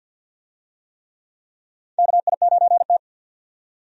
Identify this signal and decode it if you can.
Morse code, heard as a keyed tone, about 48 wpm, 700 Hz. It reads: KI9T
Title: USA Call Signs - 4 Characters - 50wpm